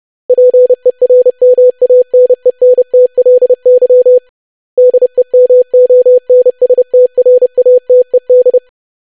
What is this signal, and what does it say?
Morse code, heard as a keyed tone, about 30 wpm, 500 Hz. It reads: PERMANENTLY DEMONSTRATED
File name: lesson / web / longwords-25wpm